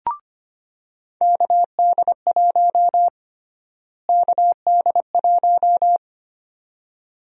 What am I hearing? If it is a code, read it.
Morse code, heard as a keyed tone, about 25 wpm, 700 Hz. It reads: KD1 KD1
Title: USA Call Sign Prefixes - Repeated 2x - 25wpm